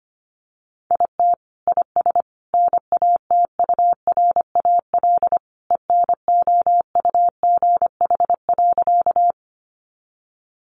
Morse code, heard as a keyed tone, about 25 wpm, 700 Hz. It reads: IT IS NATURAL ENOUGH.
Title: Sentences from Top 700 Words 25wpm